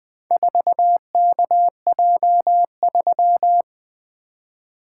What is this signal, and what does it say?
Morse code, heard as a keyed tone, about 21 wpm, 700 Hz. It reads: 4KJ3
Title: Four Letter-Number Combinations - 20wpm